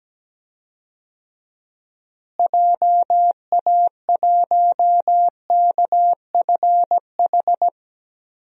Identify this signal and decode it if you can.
Morse code, heard as a keyed tone, about 17 wpm, 700 Hz. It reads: JA1KFH
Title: DXCC-100 Call Signs - 6 Characters - Repeated 3x - 17wpm